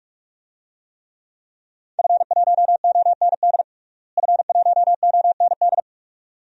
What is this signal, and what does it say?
Morse code, heard as a keyed tone, about 41 wpm, 700 Hz. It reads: F1OND F1OND
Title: DXCC-100 Call Signs - 5 Characters - Repeated 2x - 45wpm